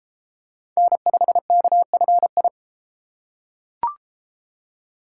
Morse code, heard as a keyed tone, about 29 wpm, 700 Hz. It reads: N5KFI